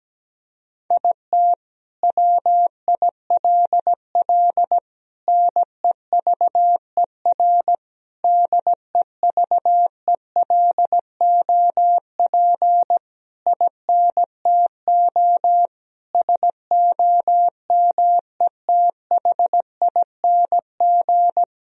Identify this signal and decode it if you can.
Morse code, heard as a keyed tone, about 17 wpm, 700 Hz. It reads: IT WILL NEVER DEVELOP INTO SOMETHING
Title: Sentences from Top 1000 Words - Review Part 2 - 17wpm